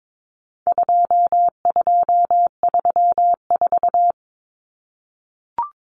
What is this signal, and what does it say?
Morse code, heard as a keyed tone, about 22 wpm, 700 Hz. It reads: 2234
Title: Common Cut Numbers - 4-Digit 22wpm